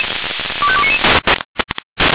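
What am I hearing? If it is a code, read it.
Five-tone selective call (ZVEI), heard as quick runs of five tones, tones of about 70 ms. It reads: 25299
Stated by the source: ZVEI five-tone call received by SDR.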